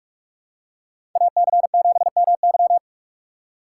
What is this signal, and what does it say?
Morse code, heard as a keyed tone, about 44 wpm, 700 Hz. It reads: AC7MY
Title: USA Call Signs - 6x Character Spacing - 45wpm